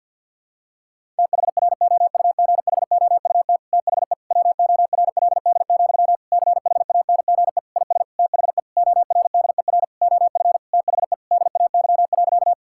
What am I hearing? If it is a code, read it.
Morse code, heard as a keyed tone, about 50 wpm, 700 Hz. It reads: THROUGHOUT THE WORLD, CHANGE IS THE ORDER OF THE DAY.
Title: Sentences from Top 1000 Words - Review Part 5 - 50wpm